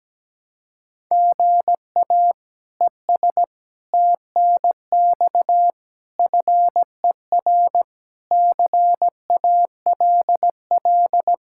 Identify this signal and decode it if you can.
Morse code, heard as a keyed tone, about 17 wpm, 700 Hz. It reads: GA ES TNX FER CALL
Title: K1USN SST QSO Practice 17wpm